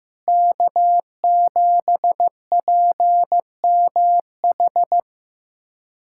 Morse code, heard as a keyed tone, about 15 wpm, 700 Hz. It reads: K7PMH
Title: USA Callsign DE Callsign - 5 Characters - 15wpm